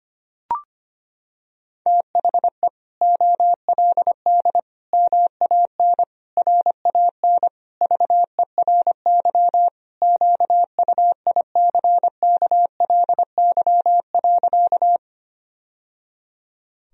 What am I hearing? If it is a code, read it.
Morse code, heard as a keyed tone, about 25 wpm, 700 Hz. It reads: THE OLD MAN RAN VERY QUICKLY.
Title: Sentences from Top 1000 Words - Review Part 3 - 25wpm